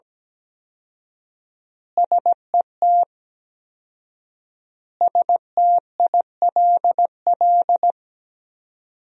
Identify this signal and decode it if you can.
Morse code, heard as a keyed tone, about 17 wpm, 700 Hz. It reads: SET STILL
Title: 4x Word-Spacing - Sentences from Top 300 Words - 17wpm